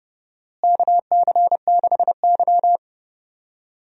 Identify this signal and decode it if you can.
Morse code, heard as a keyed tone, about 30 wpm, 700 Hz. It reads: KC6Y
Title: USA Call Signs - 4 Characters - Repeated 3x - 30wpm